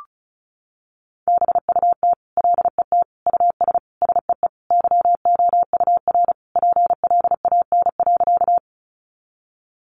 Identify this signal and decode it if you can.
Morse code, heard as a keyed tone, about 34 wpm, 700 Hz. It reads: BUT LET US SEE YOUR PLAN.
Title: Sentences from Top 500 Words 35wpm